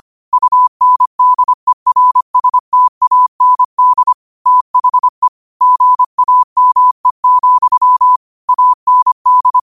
Morse code, heard as a keyed tone, about 25 wpm, 1000 Hz. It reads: UNDERSTAND THE GAME, AND
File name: qotdfiles / 25wpm